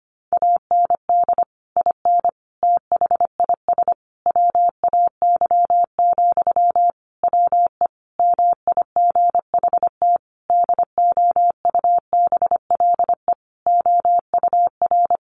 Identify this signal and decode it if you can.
Morse code, heard as a keyed tone, about 25 wpm, 700 Hz. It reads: AND IN THIS WAY, WE MIGHT DOUBLE OUR
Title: Sentences from Top 1400 Words 25wpm